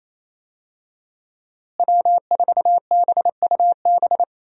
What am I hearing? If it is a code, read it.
Morse code, heard as a keyed tone, about 28 wpm, 700 Hz. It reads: W4BUB
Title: USA Call Signs - 5 Characters - Repeated 3x - 28wpm